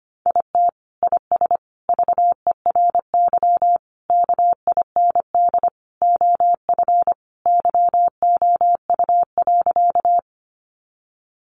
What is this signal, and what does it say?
Morse code, heard as a keyed tone, about 25 wpm, 700 Hz. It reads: IT IS VERY KIND OF YOU.
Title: Sentences from Top 200 Words 25wpm